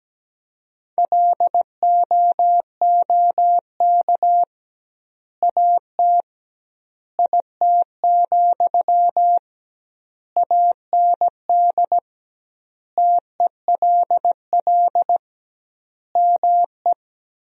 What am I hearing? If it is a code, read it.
Morse code, heard as a keyed tone, about 17 wpm, 700 Hz. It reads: LOOK AT IT, AND TELL ME
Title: 2x Word-Spacing - Sentences from Top 200 Words - 17wpm